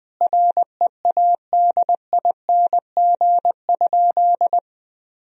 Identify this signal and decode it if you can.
Morse code, heard as a keyed tone, about 21 wpm, 700 Hz. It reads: READING?
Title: Sentences from Top 1000 Words - Review Part 3 - 20wpm